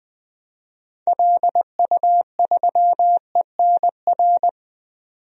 Morse code, heard as a keyed tone, about 20 wpm, 700 Hz. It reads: LU3ENR
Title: DXCC-100 Call Signs - 6 Characters - 20wpm